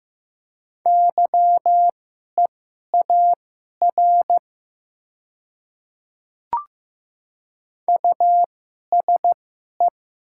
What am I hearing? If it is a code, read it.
Morse code, heard as a keyed tone, about 15 wpm, spaced as for 11 wpm, 700 Hz. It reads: YEAR USE
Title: 2x Character Spacing - Top 500 Words - 15wpm